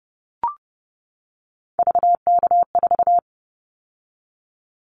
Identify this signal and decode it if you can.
Morse code, heard as a keyed tone, about 30 wpm, 700 Hz. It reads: VK4
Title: Callsign Prefixes for Top 100 DXCC - 30wpm